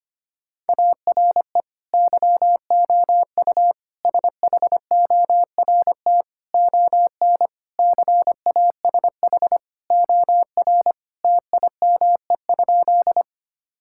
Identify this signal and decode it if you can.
Morse code, heard as a keyed tone, about 25 wpm, 700 Hz. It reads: ARE YOU SHORT ON CASH OR TIME?
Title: Sentences from Top 1000 Words - Review Part 5 - 25wpm